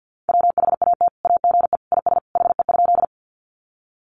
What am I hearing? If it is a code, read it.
Morse code, heard as a keyed tone, about 50 wpm, 700 Hz. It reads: WHAT AGE IS HE?
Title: Sentences from Top 500 Words 50wpm